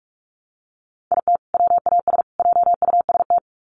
Morse code, heard as a keyed tone, about 44 wpm, 700 Hz. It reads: IT WAS JUST
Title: Sets of 3 Words 45wpm